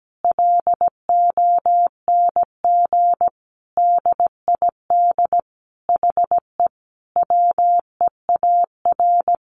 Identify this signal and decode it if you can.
Morse code, heard as a keyed tone, about 17 wpm, 700 Hz. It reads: LONG DID HE WEAR
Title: Sentences from Top 1500 Words 17wpm